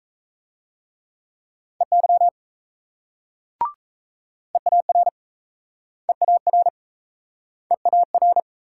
Morse code, heard as a keyed tone, about 38 wpm, 700 Hz. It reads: EY EAR EAR EAR